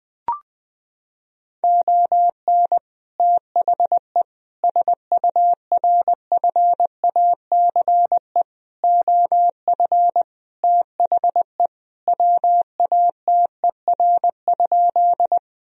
Morse code, heard as a keyed tone, about 20 wpm, 700 Hz. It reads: ON THE SURFACE OF THE WATER?
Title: Sentences from Top 1000 Words - Review Part 3 - 20wpm